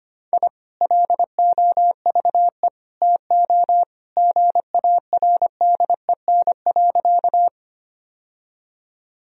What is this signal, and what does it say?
Morse code, heard as a keyed tone, about 25 wpm, 700 Hz. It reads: I LOVE TO GARDEN.
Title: Sentences from Top 1000 Words - Review Part 4 - 25wpm